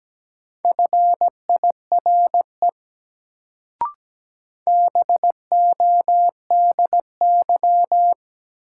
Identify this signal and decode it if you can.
Morse code, heard as a keyed tone, about 17 wpm, 700 Hz. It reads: FIRE BODY
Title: Sentences from Top 400 Words - New Word Drill - 17wpm